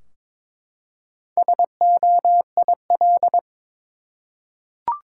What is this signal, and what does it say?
Morse code, heard as a keyed tone, about 23 wpm, 700 Hz. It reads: SOIL
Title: Four Letter Words - 22wpm